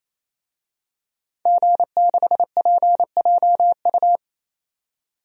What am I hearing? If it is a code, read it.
Morse code, heard as a keyed tone, about 27 wpm, 700 Hz. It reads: G6PJU
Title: DXCC-100 Call Signs - 5 Characters - 28wpm